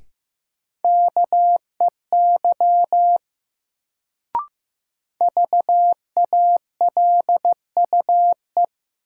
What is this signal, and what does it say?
Morse code, heard as a keyed tone, about 15 wpm, 700 Hz. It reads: KEY VALUE
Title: Sentences from Top 600 Words - New Word Drill - 15wpm